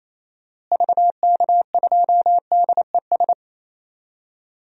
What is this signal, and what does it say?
Morse code, heard as a keyed tone, about 27 wpm, 700 Hz. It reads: VK2DES
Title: DXCC-100 Call Signs - 6 Characters - Repeated 2x - 28wpm